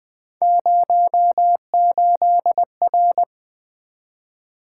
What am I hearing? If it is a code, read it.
Morse code, heard as a keyed tone, about 20 wpm, 700 Hz. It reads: 08R
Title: Three Letter-Number Combinations - 20wpm